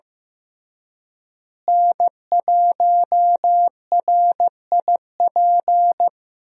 Morse code, heard as a keyed tone, about 15 wpm, 700 Hz. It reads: N1RIP
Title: USA Call Signs - 5 Characters - Repeated 3x - 15wpm